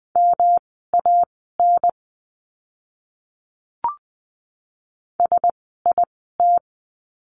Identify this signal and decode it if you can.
Morse code, heard as a keyed tone, about 20 wpm, spaced as for 14 wpm, 700 Hz. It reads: MAN SIT
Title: 108 Lesson 18f - Letter m - Words 20wpm